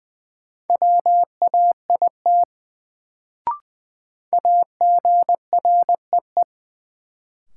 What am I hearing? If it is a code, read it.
Morse code, heard as a keyed tone, about 20 wpm, 700 Hz. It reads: WAIT AGREE
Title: Sets of 5 Words Encore - New Word Drill - 20wpm